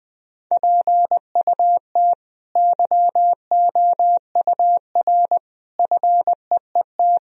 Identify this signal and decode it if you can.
Morse code, heard as a keyed tone, about 20 wpm, 700 Hz. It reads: PUT YOUR FEET
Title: Sentences from Top 1000 Words - Review Part 1 - 20wpm